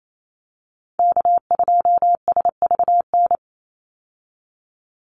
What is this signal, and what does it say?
Morse code, heard as a keyed tone, about 28 wpm, 700 Hz. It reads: K2SVN
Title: USA Call Signs 28wpm